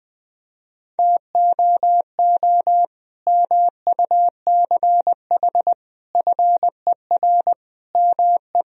Morse code, heard as a keyed tone, about 21 wpm, 700 Hz. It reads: TOO MUCH FER ME
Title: Sentences from Top 200 Words and Abbreviations 20wpm